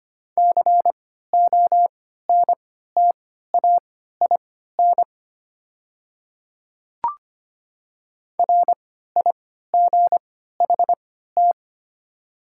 Morse code, heard as a keyed tone, about 25 wpm, spaced as for 14 wpm, 700 Hz. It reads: CONTAIN RIGHT
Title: 3x Character Spacing - Top 500 Words - 25wpm